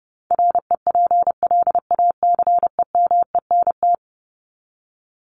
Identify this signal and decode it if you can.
Morse code, heard as a keyed tone, about 30 wpm, 700 Hz. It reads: REPLACEMENT
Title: Top 1751-2000 Words 30wpm